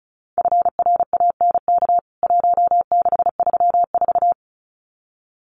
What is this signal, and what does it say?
Morse code, heard as a keyed tone, about 34 wpm, 700 Hz. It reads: FRANK 1634
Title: CWT Exchange 35wpm